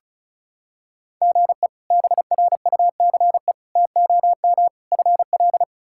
Morse code, heard as a keyed tone, about 36 wpm, 700 Hz. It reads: GE BRUCE TOM FL
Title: K1USN SST QSO Practice - 35wpm